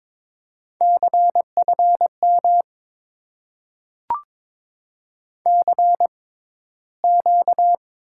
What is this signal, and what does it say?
Morse code, heard as a keyed tone, about 22 wpm, spaced as for 14 wpm, 700 Hz. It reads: CFM CQ
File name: Top 90 QSO Elements - 6x Character Spacing - 22wpm